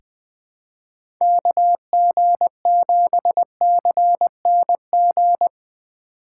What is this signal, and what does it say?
Morse code, heard as a keyed tone, about 20 wpm, 700 Hz. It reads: KG7CNG
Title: USA Call Signs - 6 Characters - 3x Character Spacing - 20wpm